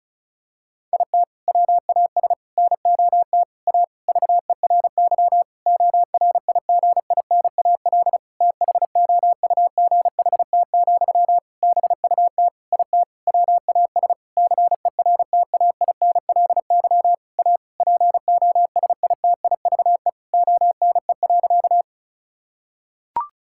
Morse code, heard as a keyed tone, about 34 wpm, 700 Hz. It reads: IT WAS NOT A VERY ORIGINAL THOUGHT, BUT IT WAS CERTAINLY A POSITIVE ONE.
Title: Sentences from Top 1000 Words - Review Part 6 - 35wpm